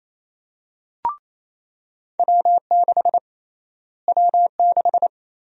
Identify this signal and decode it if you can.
Morse code, heard as a keyed tone, about 27 wpm, 700 Hz. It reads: W6 W6
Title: USA Call Sign Prefixes - Repeated 2x - 28wpm